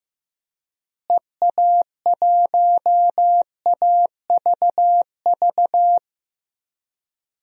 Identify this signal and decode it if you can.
Morse code, heard as a keyed tone, about 15 wpm, 700 Hz. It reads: EA1AVV
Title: DXCC-100 Call Signs - 6 Characters - Repeated 3x - 15wpm